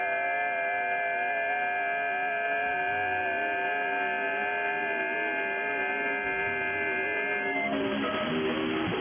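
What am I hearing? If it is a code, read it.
FT8.
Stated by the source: FT8 signal received on an IC-7100 radio.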